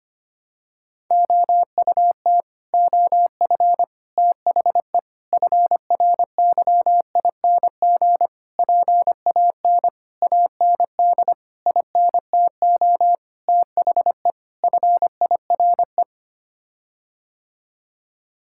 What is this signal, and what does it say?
Morse code, heard as a keyed tone, about 25 wpm, 700 Hz. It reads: OUT OF THE FRYING PAN AND INTO THE FIRE
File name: Sets of 9 Words - 25wpm